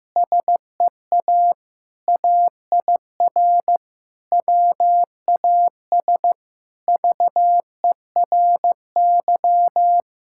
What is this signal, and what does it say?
Morse code, heard as a keyed tone, about 16 wpm, 700 Hz. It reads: SEA AIR WAS VERY
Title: Sentences from Top 1600 Words 15wpm